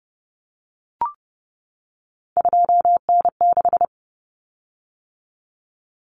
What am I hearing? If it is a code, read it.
Morse code, heard as a keyed tone, about 30 wpm, 700 Hz. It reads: 2N6
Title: Common Cut Numbers - 3-Digit 30wpm